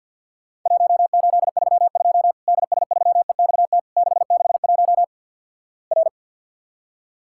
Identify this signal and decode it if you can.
Morse code, heard as a keyed tone, about 50 wpm, 700 Hz. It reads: 1832 DS3EXT 661 R
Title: Worked All Europe DX Contest - QTC Exchange - 50wpm